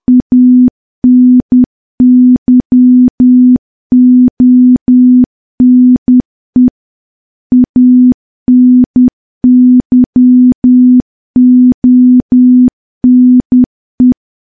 Morse code, heard as a keyed tone, about 10 wpm, 250 Hz. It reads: ANYONE ANYONE
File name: morse / morse-training-10WPM